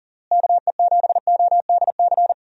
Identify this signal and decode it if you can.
Morse code, heard as a keyed tone, about 40 wpm, 700 Hz. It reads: KE7ODC
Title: USA Call Signs - 6 Characters - Repeated 3x - 40wpm